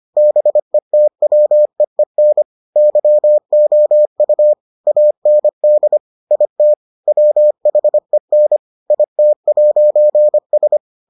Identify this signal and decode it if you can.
Morse code, heard as a keyed tone, about 25 wpm, 600 Hz. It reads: BETWEEN YOU AND IT WHEN IT'S